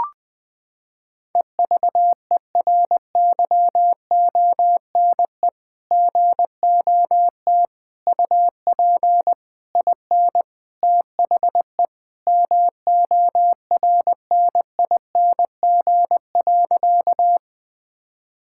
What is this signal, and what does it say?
Morse code, heard as a keyed tone, about 21 wpm, 700 Hz. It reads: EVERYONE GOT UP IN THE MORNING.
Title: Sentences from Top 1000 Words - Review Part 5 - 20wpm